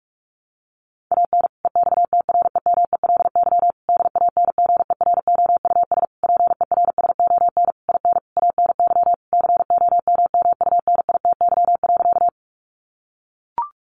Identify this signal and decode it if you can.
Morse code, heard as a keyed tone, about 44 wpm, 700 Hz. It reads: AN EXTREMELY DANGEROUS PERSON IN ANY COMMUNITY.
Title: Sentences from Top 1200 Words 45wpm